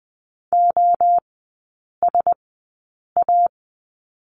Morse code, heard as a keyed tone, about 20 wpm, spaced as for 8 wpm, 700 Hz. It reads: OSA
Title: 035 Lesson 7e - Letter s - Three Characters 20wpm